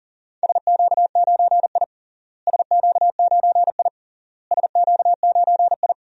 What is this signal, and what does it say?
Morse code, heard as a keyed tone, about 40 wpm, 700 Hz. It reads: SQ9I SQ9I SQ9I
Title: DXCC-100 Call Signs - 4 Characters - Repeated 3x - 40wpm